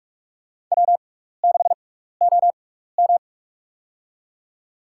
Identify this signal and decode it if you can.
Morse code, heard as a keyed tone, about 48 wpm, spaced as for 16 wpm, 700 Hz. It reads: W6OM
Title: USA Call Signs - 4 Characters - 6x Character Spacing - 45wpm